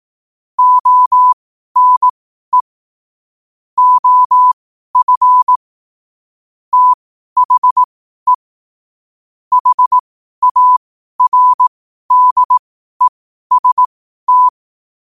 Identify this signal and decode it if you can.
Morse code, heard as a keyed tone, about 18 wpm, spaced as for 12 wpm, 1000 Hz. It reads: ONE OF THE HARDEST
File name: qotdfiles / 12wpm